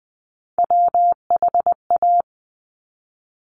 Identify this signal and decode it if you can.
Morse code, heard as a keyed tone, about 20 wpm, 700 Hz. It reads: WHA
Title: Top Three Letter Combinations in Words 20wpm